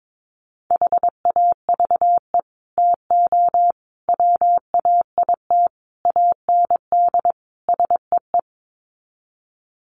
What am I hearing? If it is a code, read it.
Morse code, heard as a keyed tone, about 22 wpm, 700 Hz. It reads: HAVE TO WAIT AND SEE
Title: Sets of 5 Words Encore2 22wpm